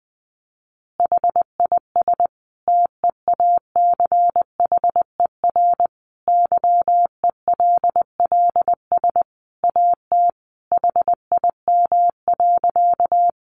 Morse code, heard as a keyed tone, about 20 wpm, 700 Hz. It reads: HIS TEACHER YELLS AT HIM.
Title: Additional Easy Phrases 20wpm